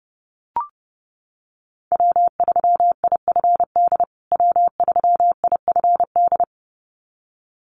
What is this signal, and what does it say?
Morse code, heard as a keyed tone, about 30 wpm, 700 Hz. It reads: W3IFD W3IFD
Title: USA Call Signs Repeated 2x 30wpm